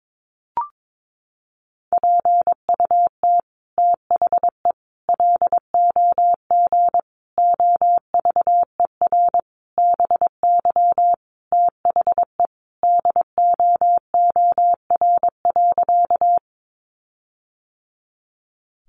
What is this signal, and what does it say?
Morse code, heard as a keyed tone, about 22 wpm, 700 Hz. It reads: PUT THE LOG OVER BY THE DOOR.
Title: Sentences from Top 1400 Words 22wpm